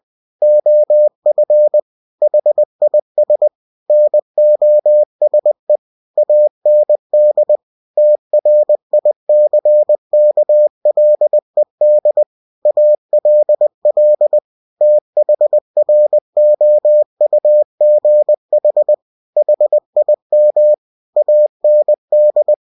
Morse code, heard as a keyed tone, about 20 wpm, 600 Hz. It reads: OF HIS NOSE AND TRICKLED ALL THROUGH HIM AND
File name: cw-books / winnie-the-pooh / winnie-20wpm-page-0049